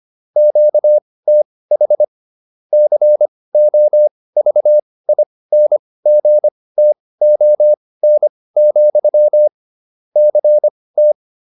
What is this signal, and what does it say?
Morse code, heard as a keyed tone, about 25 wpm, spaced as for 18 wpm, 600 Hz. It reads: QTH COVINGTON, CT
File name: QSO403 18